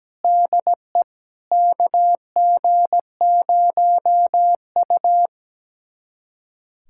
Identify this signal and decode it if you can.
Morse code, heard as a keyed tone, about 17 wpm, 700 Hz. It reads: DE KG0U
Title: USA Callsign DE Callsign - 4 Characters - 17wpm